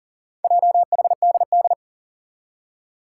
Morse code, heard as a keyed tone, about 40 wpm, 700 Hz. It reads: JHDD
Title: Four Challenging Letter-Number Combinations 40wpm